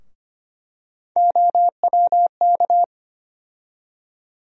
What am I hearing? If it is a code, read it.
Morse code, heard as a keyed tone, about 25 wpm, 700 Hz. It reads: OWK
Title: USA Call Sign Postfixes - 3 Letters 25wpm